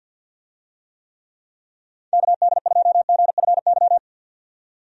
Morse code, heard as a keyed tone, about 52 wpm, 700 Hz. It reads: KD2GFY
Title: USA Call Signs - 6 Characters - 50wpm